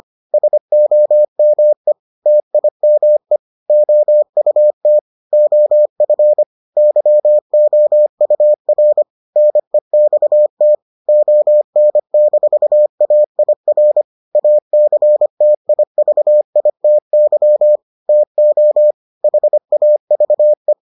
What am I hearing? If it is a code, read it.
Morse code, heard as a keyed tone, about 25 wpm, 600 Hz. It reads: SOME TIME OUT OF YOUR NEXT ON-AIR ACTIVITY TO HAVE